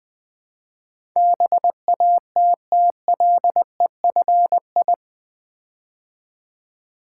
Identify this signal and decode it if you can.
Morse code, heard as a keyed tone, about 20 wpm, 700 Hz. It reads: BATTLEFI
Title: Build A Word - 4 Letters At A Time - 11-Letter Words - 20wpm